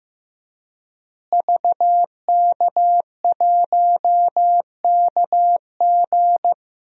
Morse code, heard as a keyed tone, about 15 wpm, 700 Hz. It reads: VK1KG
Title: DXCC-100 Call Signs - 5 Characters - Repeated 2x - 15wpm